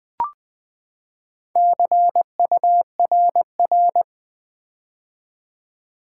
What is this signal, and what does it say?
Morse code, heard as a keyed tone, about 20 wpm, 700 Hz. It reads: CURR
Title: Build A Word - 4 Letters At A Time - 8-Letter Words - 20wpm